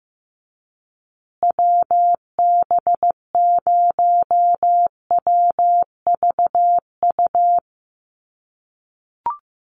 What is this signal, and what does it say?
Morse code, heard as a keyed tone, about 15 wpm, 700 Hz. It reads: WB0WVU
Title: USA Call Signs Repeated 2x 15wpm